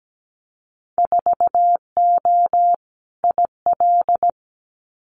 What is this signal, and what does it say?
Morse code, heard as a keyed tone, about 17 wpm, 700 Hz. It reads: 4O IL
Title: Winter Field Day Exchanges Encore 17wpm